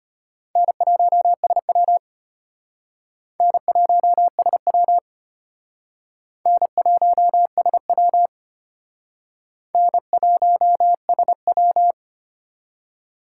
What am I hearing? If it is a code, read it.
Morse code, heard as a keyed tone, about 33 wpm, 700 Hz. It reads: N1SW N1SW N1SW N1SW